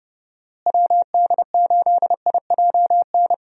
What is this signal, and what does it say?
Morse code, heard as a keyed tone, about 32 wpm, 700 Hz. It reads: WD8IJN
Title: USA Call Signs - 6 Characters - Repeated 2x - 30wpm